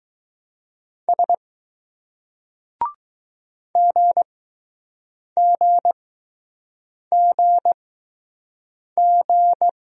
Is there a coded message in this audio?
Yes: Morse code from 1.1 to 9.7 s, heard as a keyed tone, about 20 wpm, spaced as for 6 wpm, 700 Hz. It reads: S GGGG